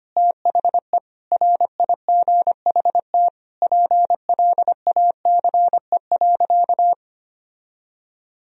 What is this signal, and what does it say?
Morse code, heard as a keyed tone, about 25 wpm, 700 Hz. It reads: THE RIGHT PLACE.
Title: Sentences from Top 1000 Words - Review Part 3 - 25wpm